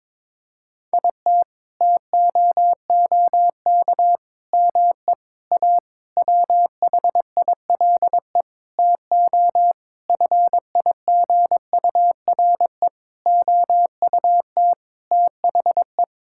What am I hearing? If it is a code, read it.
Morse code, heard as a keyed tone, about 22 wpm, 700 Hz. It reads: IT TOOK ME A WHILE TO FIGURE OUT THE
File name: practice / 22wpm / Sentences with 8-Word Phrases - 22wpm